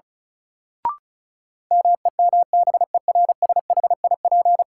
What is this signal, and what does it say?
Morse code, heard as a keyed tone, about 36 wpm, 700 Hz. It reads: MEMBERSHIP
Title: Sentences from Top 1400 Words - New Word Drill - 35wpm